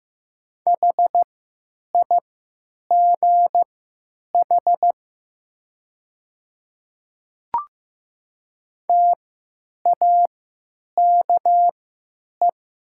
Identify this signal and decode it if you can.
Morse code, heard as a keyed tone, about 15 wpm, spaced as for 9 wpm, 700 Hz. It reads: HIGH TAKE
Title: 3x Character Spacing - Top 200 Words - 15wpm